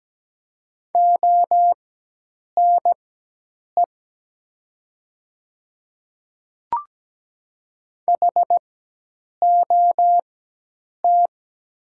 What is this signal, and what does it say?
Morse code, heard as a keyed tone, about 17 wpm, spaced as for 8 wpm, 700 Hz. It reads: ONE HOT
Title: 4x Character Spacing - Top 300 Words - 17wpm